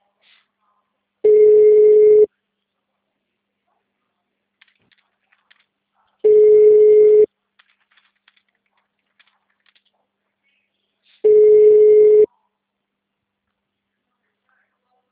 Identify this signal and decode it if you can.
Ringback tone (Europe, 425 Hz).
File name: out- -2045- - - .4253